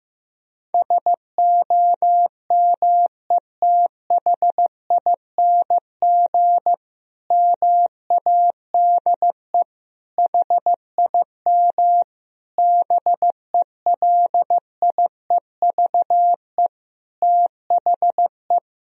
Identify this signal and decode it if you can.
Morse code, heard as a keyed tone, about 15 wpm, 700 Hz. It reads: SOMETHING MADE HIM BELIEVE THE
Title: Sentences from Top 1000 Words - Review Part 1 - 15wpm